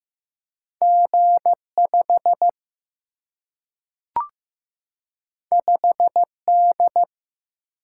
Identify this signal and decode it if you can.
Morse code, heard as a keyed tone, about 15 wpm, 700 Hz. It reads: G5 5D
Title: Single-Letter Single-Number 15wpm